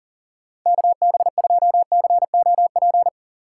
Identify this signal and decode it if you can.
Morse code, heard as a keyed tone, about 50 wpm, 700 Hz. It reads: KB2COP